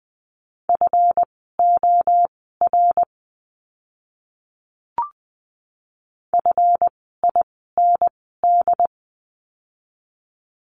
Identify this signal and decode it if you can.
Morse code, heard as a keyed tone, about 20 wpm, spaced as for 14 wpm, 700 Hz. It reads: FOR FIND
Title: 192 Lesson 31f - Letter v - Words 20wpm